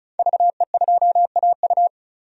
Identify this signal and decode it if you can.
Morse code, heard as a keyed tone, about 36 wpm, 700 Hz. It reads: VE2AU
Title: DXCC-100 Call Signs - 5 Characters - 35wpm